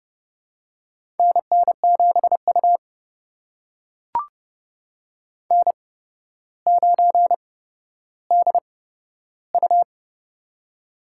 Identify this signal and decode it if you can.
Morse code, heard as a keyed tone, about 30 wpm, spaced as for 13 wpm, 700 Hz. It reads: NN7U N9DU
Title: USA Call Signs - 4 Characters - 8x Character Spacing - 30wpm